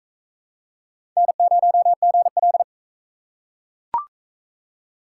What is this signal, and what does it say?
Morse code, heard as a keyed tone, about 41 wpm, 700 Hz. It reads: N0GL